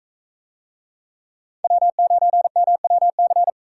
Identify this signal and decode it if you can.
Morse code, heard as a keyed tone, about 44 wpm, 700 Hz. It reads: W9MWC